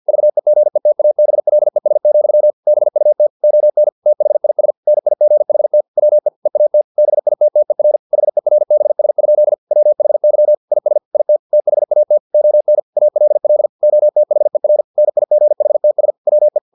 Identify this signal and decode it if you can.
Morse code, heard as a keyed tone, about 50 wpm, 600 Hz. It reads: VEGETABLES, BUT ON THIS NIGHT WE EAT BITTER HERBS? WHY IS IT THAT ON ALL OTHER NIGHTS WE